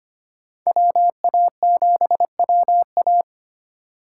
Morse code, heard as a keyed tone, about 25 wpm, 700 Hz. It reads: WA7WA
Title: USA Call Signs - 5 Characters - 2x Character Spacing - 25wpm